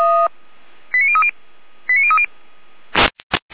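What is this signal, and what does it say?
Five-tone selective call (ZVEI), heard as quick runs of five tones, tones of about 70 ms. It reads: 89030 89030
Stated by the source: Received with an SDR; ZVEI five-tone selective call.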